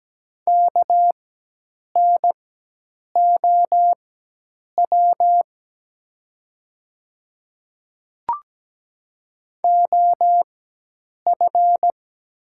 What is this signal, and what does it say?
Morse code, heard as a keyed tone, about 17 wpm, spaced as for 8 wpm, 700 Hz. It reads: KNOW OF
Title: 4x Character Spacing - Top 100 Words - 17wpm